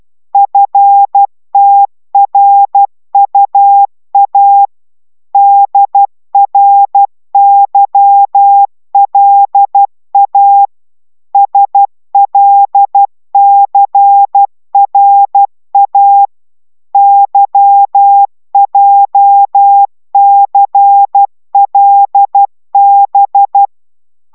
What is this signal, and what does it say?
Morse code, heard as a keyed tone, about 12 wpm, 800 Hz. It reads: FTRUA DRYLA SLCRA YJCLB